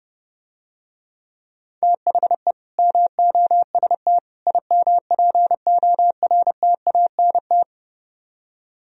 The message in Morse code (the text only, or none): THE MOST IMPORTANT